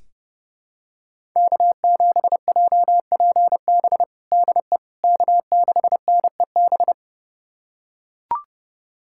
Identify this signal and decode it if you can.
Morse code, heard as a keyed tone, about 30 wpm, 700 Hz. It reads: K7JPB DE K6NEB
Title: USA Callsign DE Callsign 30wpm